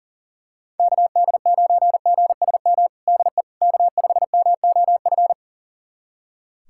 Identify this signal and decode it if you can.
Morse code, heard as a keyed tone, about 40 wpm, 700 Hz. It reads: KD9GSM DE K5MOF
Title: USA Callsign DE Callsign 40wpm